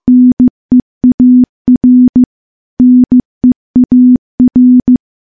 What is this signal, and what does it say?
Morse code, heard as a keyed tone, about 16 wpm, 250 Hz. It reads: NEAR NEAR
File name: morse-training-15WPM